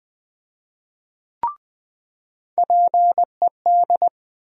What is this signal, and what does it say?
Morse code, heard as a keyed tone, about 20 wpm, 700 Hz. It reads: PED
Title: USA Call Sign Postfixes - 3 Letters 20wpm